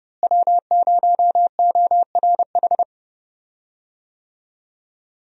Morse code, heard as a keyed tone, about 30 wpm, 700 Hz. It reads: W0ORH
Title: USA Call Signs - 5 Characters - Repeated 3x - 30wpm